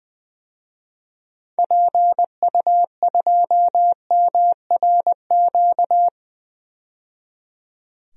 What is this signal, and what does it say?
Morse code, heard as a keyed tone, about 20 wpm, 700 Hz. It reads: PU2MRQ
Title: DXCC-100 Call Signs - 6 Characters - 20wpm